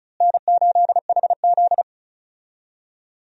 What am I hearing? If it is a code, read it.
Morse code, heard as a keyed tone, about 34 wpm, 700 Hz. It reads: N8HZ
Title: USA Call Signs - 4 Characters - Repeated 3x - 35wpm